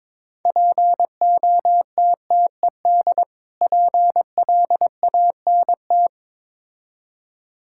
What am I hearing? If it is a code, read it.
Morse code, heard as a keyed tone, about 22 wpm, 690 Hz. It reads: POTTED PLANT
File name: Difficult Words with Hints - 22wpm